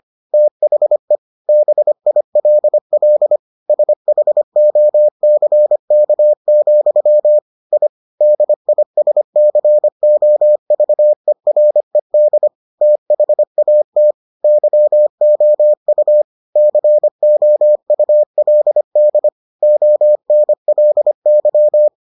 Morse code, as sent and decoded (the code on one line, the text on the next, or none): - .... / -... .. .-.. .-.. / ... .... --- -.-. -.- --..-- / .. / -.. .. ... -.-. --- ...- . .-. -.. / - .... .- - / -.-- --- ..- / -.-. --- ..- .-.. -.. / --- -. .-.. -.--
THE BILL SHOCK, I DISCOVERED THAT YOU COULD ONLY